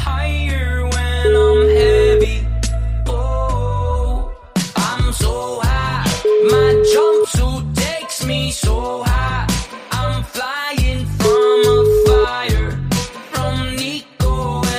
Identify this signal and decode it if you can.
Ringback tone (Europe, 425 Hz), with noise.